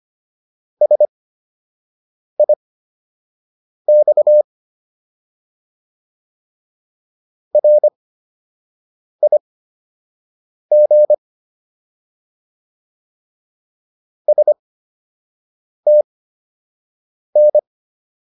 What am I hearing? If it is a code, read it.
Morse code, heard as a keyed tone, about 25 wpm, spaced as for 6 wpm, 600 Hz. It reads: SIX RIG STN